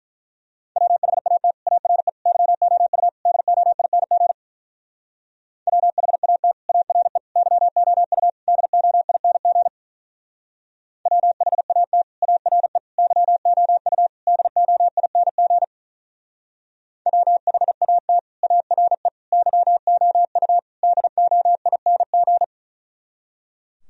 Morse code, heard as a keyed tone, about 42 wpm, 700 Hz. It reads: WHAT ARE YOU DOING WHAT ARE YOU DOING WHAT ARE YOU DOING WHAT ARE YOU DOING
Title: Speed-Racing - Sets of 4 Words 35wpm